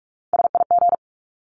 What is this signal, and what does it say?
Morse code, heard as a keyed tone, about 44 wpm, 700 Hz. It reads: SIG
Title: Abbreviations 45wpm